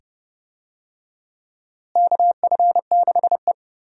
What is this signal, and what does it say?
Morse code, heard as a keyed tone, about 30 wpm, 700 Hz. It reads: KF6E
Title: USA Call Signs - 4 Characters - Repeated 2x - 30wpm